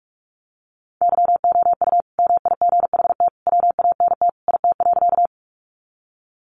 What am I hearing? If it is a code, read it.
Morse code, heard as a keyed tone, about 44 wpm, 700 Hz. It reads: YOU MIGHT WANT IT.
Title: Sentences from Top 300 Words 45wpm